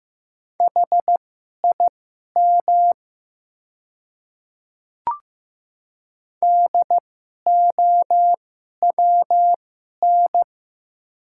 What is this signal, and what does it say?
Morse code, heard as a keyed tone, about 15 wpm, spaced as for 11 wpm, 700 Hz. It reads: HIM DOWN
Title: 2x Character Spacing - Top 100 Words - 15wpm